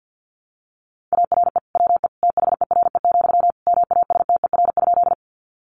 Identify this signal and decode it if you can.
Morse code, heard as a keyed tone, about 50 wpm, 700 Hz. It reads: ARE WE THERE, MASTER?
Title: Sentences from Top 1300 Words 50wpm